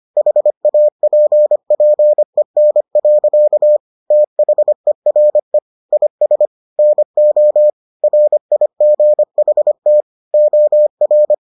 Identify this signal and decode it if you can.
Morse code, heard as a keyed tone, about 25 wpm, 600 Hz. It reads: HAPPEN. THERE IS NO RIGHT OR